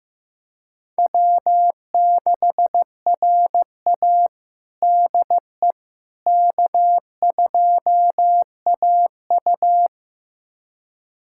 Morse code, heard as a keyed tone, about 15 wpm, 700 Hz. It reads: W6RA DE K2AU
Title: USA Callsign DE Callsign - 4 Characters - 15wpm